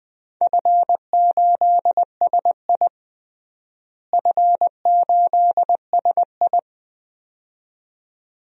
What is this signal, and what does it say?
Morse code, heard as a keyed tone, about 20 wpm, 700 Hz. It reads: F8SI F8SI
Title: DXCC-100 Call Signs - 4 Characters - Repeated 2x - 20wpm